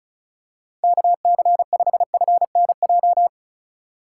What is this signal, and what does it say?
Morse code, heard as a keyed tone, about 34 wpm, 700 Hz. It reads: KC5FNJ
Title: USA Call Signs - 6 Characters - 3x Character Spacing - 35wpm